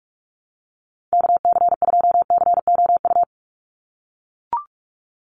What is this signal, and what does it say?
Morse code, heard as a keyed tone, about 44 wpm, 700 Hz. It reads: KC2COU
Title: USA Call Signs 45wpm